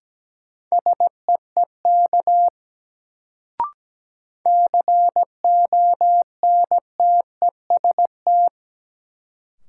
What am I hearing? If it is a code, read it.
Morse code, heard as a keyed tone, about 17 wpm, 700 Hz. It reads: SEEK CONTEST
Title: Sentences from Top 1600 Words - New Word Drill - 17wpm